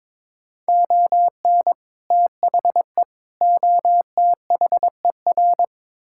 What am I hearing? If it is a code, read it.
Morse code, heard as a keyed tone, about 22 wpm, 700 Hz. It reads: ON THE OTHER
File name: practice / 22wpm / Build A Phrase - 5-Word Phrases - 22wpm